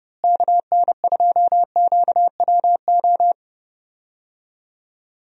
Morse code, heard as a keyed tone, about 30 wpm, 700 Hz. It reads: KN2QWO
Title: USA Call Signs - 6 Characters - Repeated 2x - 30wpm